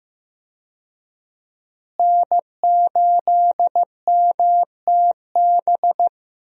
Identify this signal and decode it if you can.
Morse code, heard as a keyed tone, about 15 wpm, 700 Hz. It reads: N8MTB
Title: USA Call Signs - 5 Characters - Repeated 3x - 15wpm